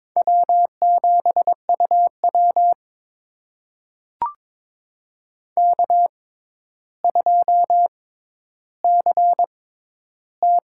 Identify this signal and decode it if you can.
Morse code, heard as a keyed tone, about 22 wpm, spaced as for 11 wpm, 700 Hz. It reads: W7UW K2CT
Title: USA Call Signs - 4 Characters - 6x Character Spacing - 22wpm